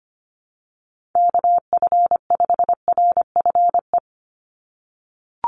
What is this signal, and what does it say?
Morse code, heard as a keyed tone, about 25 wpm, 700 Hz. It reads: KF5RFE
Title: USA Call Signs 25wpm